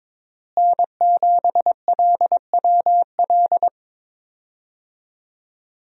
Morse code, heard as a keyed tone, about 22 wpm, 700 Hz. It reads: N7LWL
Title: USA Call Signs - 5 Characters - Repeated 3x - 22wpm